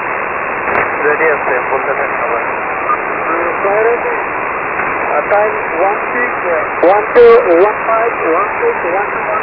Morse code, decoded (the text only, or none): none